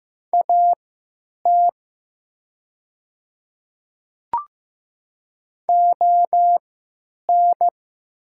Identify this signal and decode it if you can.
Morse code, heard as a keyed tone, about 15 wpm, spaced as for 9 wpm, 690 Hz. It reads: AT ON
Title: 3x Character Spacing - Top 100 Words - 15wpm